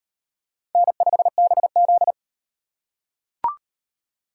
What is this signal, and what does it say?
Morse code, heard as a keyed tone, about 38 wpm, 700 Hz. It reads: N5BZ